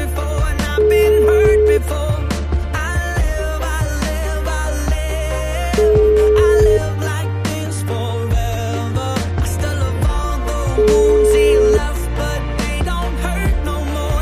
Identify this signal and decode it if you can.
Ringback tone (Europe, 425 Hz), with noise.